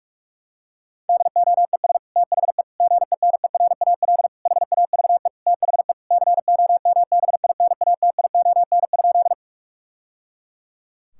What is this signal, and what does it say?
Morse code, heard as a keyed tone, about 41 wpm, 700 Hz. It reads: DOES THE GENERAL HAVE THE COMBINATION?
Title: Sentences from Top 1000 Words - Review Part 6 - 45wpm